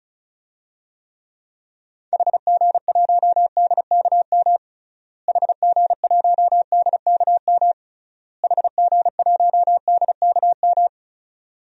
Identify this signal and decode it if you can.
Morse code, heard as a keyed tone, about 34 wpm, 700 Hz. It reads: HG1DKM HG1DKM HG1DKM
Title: DXCC-100 Call Signs - 6 Characters - Repeated 3x - 35wpm